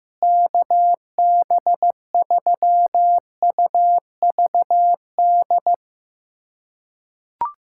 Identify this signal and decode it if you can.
Morse code, heard as a keyed tone, about 15 wpm, 700 Hz. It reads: KB3UVD
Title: USA Callsign DE Callsign 15wpm